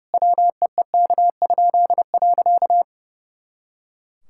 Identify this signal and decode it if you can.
Morse code, heard as a keyed tone, about 30 wpm, 700 Hz. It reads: WEEK?.
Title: Pangram Challenge v2 - 30wpm